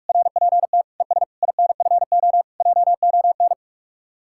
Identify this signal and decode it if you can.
Morse code, heard as a keyed tone, about 54 wpm, 700 Hz. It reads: RPT ES INFO JON
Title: Speed-Racing Common QSO Phrases - 45wpm